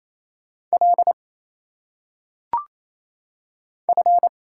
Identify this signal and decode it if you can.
Morse code, heard as a keyed tone, about 27 wpm, 700 Hz. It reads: L F
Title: Single Letters - Phonetic Pronunciation 28wpm